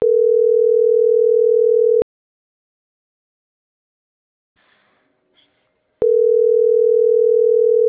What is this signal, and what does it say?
Ringback tone (North America), 440 + 480 Hz.